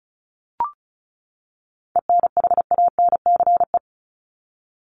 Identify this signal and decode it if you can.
Morse code, heard as a keyed tone, about 34 wpm, 700 Hz. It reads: ENHANCE
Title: Top 1251-1500 Words 35wpm